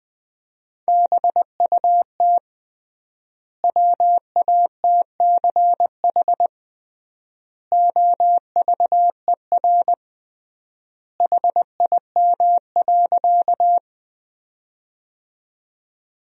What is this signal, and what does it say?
Morse code, heard as a keyed tone, about 20 wpm, 700 Hz. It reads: BUT WATCH OVER HIM.
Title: 3x Word-Spacing - Sentences from Top 300 Words - 20wpm